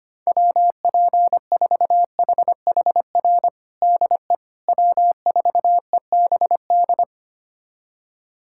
Morse code, heard as a keyed tone, about 25 wpm, 700 Hz. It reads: WP4HHR DE W4EBD
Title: USA Callsign DE Callsign 25wpm